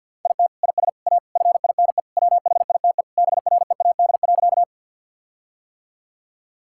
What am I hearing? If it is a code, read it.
Morse code, heard as a keyed tone, about 50 wpm, 700 Hz. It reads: IT IS A FINE WHITE BREAD.
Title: Sentences from Top 1600 Words 50wpm